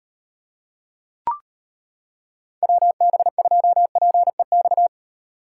Morse code, heard as a keyed tone, about 31 wpm, 700 Hz. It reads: WB2PEX
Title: Speed Racing - USA 2x3 Call Signs 25wpm